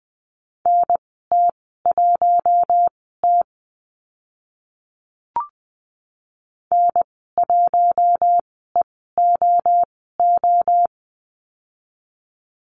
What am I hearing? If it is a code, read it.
Morse code, heard as a keyed tone, about 20 wpm, spaced as for 14 wpm, 700 Hz. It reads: NT1T N1EOO
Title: 048 Lesson 9g - Number 4 - Call Signs 20wpm